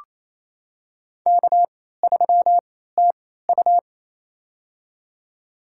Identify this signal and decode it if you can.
Morse code, heard as a keyed tone, about 29 wpm, spaced as for 16 wpm, 700 Hz. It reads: K3TU
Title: USA Call Signs - 3x Character Spacing - 28wpm